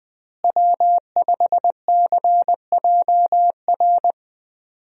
Morse code, heard as a keyed tone, about 20 wpm, 700 Hz. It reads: W5CJR
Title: USA Call Signs - 5 Characters - Repeated 3x - 20wpm